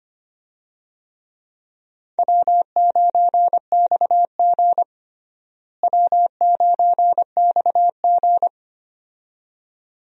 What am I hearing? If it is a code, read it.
Morse code, heard as a keyed tone, about 25 wpm, 700 Hz. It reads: W9XG W9XG
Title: USA Call Signs - 4 Characters - Repeated 2x - 25wpm